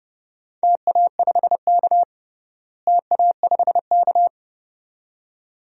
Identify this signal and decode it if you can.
Morse code, heard as a keyed tone, about 32 wpm, 700 Hz. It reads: TA5K TA5K
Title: DXCC-100 Call Signs - 4 Characters - Repeated 2x - 30wpm